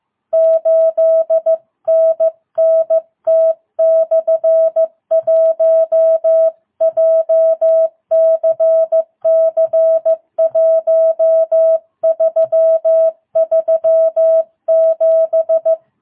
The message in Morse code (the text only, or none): 8NNT/1 JCC 1337